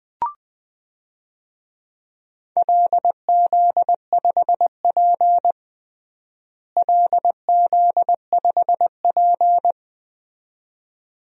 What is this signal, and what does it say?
Morse code, heard as a keyed tone, about 21 wpm, 700 Hz. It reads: LZ5P LZ5P
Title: DXCC-100 Call Signs - 4 Characters - Repeated 2x - 20wpm